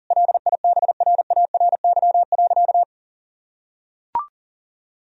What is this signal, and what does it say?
Morse code, heard as a keyed tone, about 40 wpm, 700 Hz. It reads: LIBRARY.
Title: Sentences from Top 1000 Words - Review Part 4 - 40wpm